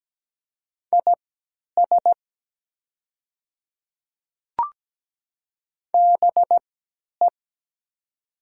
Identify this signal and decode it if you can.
Morse code, heard as a keyed tone, about 17 wpm, spaced as for 10 wpm, 700 Hz. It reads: IS BE
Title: 3x Character Spacing - Top 100 Words - 17wpm